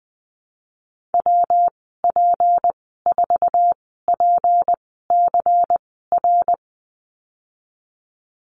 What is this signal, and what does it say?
Morse code, heard as a keyed tone, about 20 wpm, spaced as for 15 wpm, 700 Hz. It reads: WP4PCR
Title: 179 Lesson 29g - Slash - Call Signs 20wpm